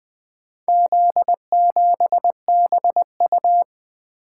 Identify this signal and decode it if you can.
Morse code, heard as a keyed tone, about 20 wpm, 700 Hz. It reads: Z7BU
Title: Four Letter-Number Combinations - 20wpm